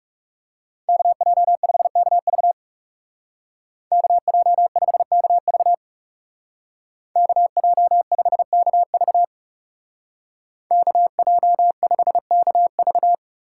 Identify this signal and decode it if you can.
Morse code, heard as a keyed tone, about 37 wpm, 700 Hz. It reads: KJ5KV KJ5KV KJ5KV KJ5KV
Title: Speed Racing - USA 2x2 Call Signs 30wpm